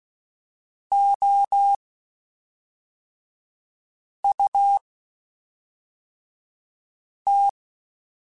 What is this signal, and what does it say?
Morse code, heard as a keyed tone, about 16 wpm, spaced as for 3 wpm, 800 Hz. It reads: OUT
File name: General / General-mp3s / Houses @5